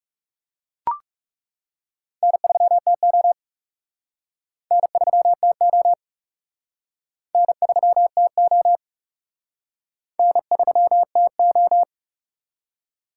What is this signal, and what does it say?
Morse code, heard as a keyed tone, about 40 wpm, 700 Hz. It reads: N3TO N3TO N3TO N3TO